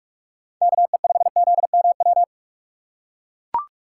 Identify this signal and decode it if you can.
Morse code, heard as a keyed tone, about 42 wpm, 700 Hz. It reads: KE5ZMW